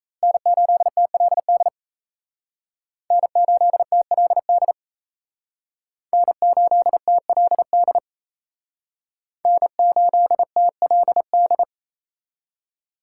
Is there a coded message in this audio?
Yes: Morse code from 0.2 to 11.6 s, heard as a keyed tone, about 36 wpm, 700 Hz. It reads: N8TLD N8TLD N8TLD N8TLD